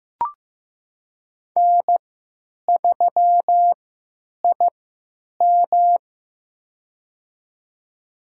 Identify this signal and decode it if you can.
Morse code, heard as a keyed tone, about 15 wpm, spaced as for 9 wpm, 700 Hz. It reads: N3IM
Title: USA Call Signs - 4 Characters - 3x Character Spacing - 15wpm